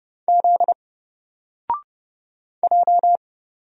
Morse code, heard as a keyed tone, about 30 wpm, 700 Hz. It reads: Z J
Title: Single Character (Letter:Number:Punctuation:Prosign) 30wpm